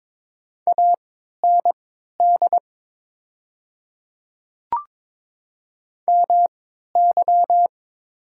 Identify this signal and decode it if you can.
Morse code, heard as a keyed tone, about 22 wpm, spaced as for 13 wpm, 700 Hz. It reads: AND MY
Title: 3x Character Spacing - Top 100 Words - 22wpm